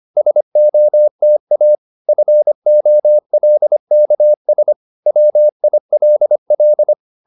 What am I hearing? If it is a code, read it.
Morse code, heard as a keyed tone, about 25 wpm, 600 Hz. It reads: SOTA FOLKS WILL